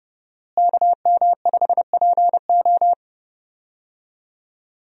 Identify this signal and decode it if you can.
Morse code, heard as a keyed tone, about 32 wpm, 700 Hz. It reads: KM5PO
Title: USA Call Signs - 5 Characters - Repeated 2x - 30wpm